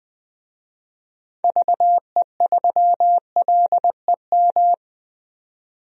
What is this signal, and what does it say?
Morse code, heard as a keyed tone, about 21 wpm, 700 Hz. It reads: VE3LEM
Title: DXCC-100 Call Signs - 6 Characters - 20wpm